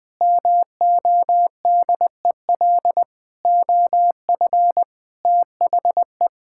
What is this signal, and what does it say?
Morse code, heard as a keyed tone, about 20 wpm, 700 Hz. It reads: MODEL OF THE
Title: Sentences from Top 1000 Words - Review Part 5 - 20wpm